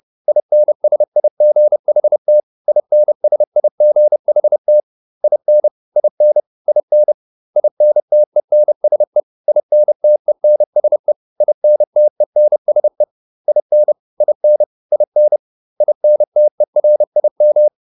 Morse code, heard as a keyed tone, about 30 wpm, 600 Hz. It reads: INSIGHT INSIGHT IN IN IN INTENSE INTENSE INTENSE IN IN IN INTERIM